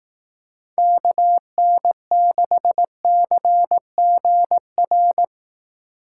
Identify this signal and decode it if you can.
Morse code, heard as a keyed tone, about 18 wpm, 700 Hz. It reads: KN6CGR
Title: Speed Racing - USA 2x3 Call Signs 15wpm